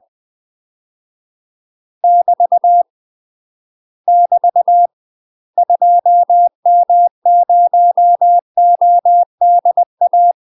Morse code, heard as a keyed tone, about 20 wpm, 700 Hz. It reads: <BT> <BT> 2M0ODA
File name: UK MASTER QSOs TRAINING / 20WPM / UK MASTER QSOs 0064